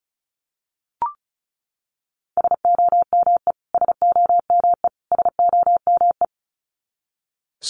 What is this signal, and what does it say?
Morse code, heard as a keyed tone, about 34 wpm, 700 Hz. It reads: SOME SOME SOME
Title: Top 300 Words Repeated 35wpm